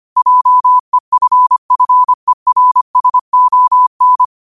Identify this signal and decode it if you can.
Morse code, heard as a keyed tone, about 25 wpm, 1000 Hz. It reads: JEFFERSON
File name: qotdfiles / 25wpm